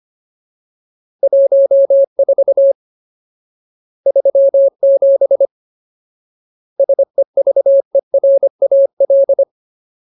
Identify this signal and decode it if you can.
Morse code, heard as a keyed tone, about 25 wpm, 550 Hz. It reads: 14 37 SEVERAL
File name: Ultimate Word List / 25WPM Ultimate